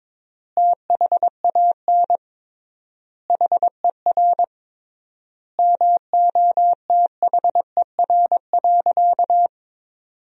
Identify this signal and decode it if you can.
Morse code, heard as a keyed tone, about 21 wpm, 700 Hz. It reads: THAN HER MOTHER.
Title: 3x Word-Spacing - Sentences from Top 400 Words - 22wpm